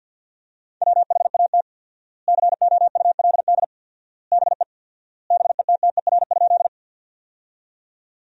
Morse code, heard as a keyed tone, about 50 wpm, 700 Hz. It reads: WHAT COULD BE BETTER?
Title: 4x Word-Spacing - Sentences from Top 400 Words - 50wpm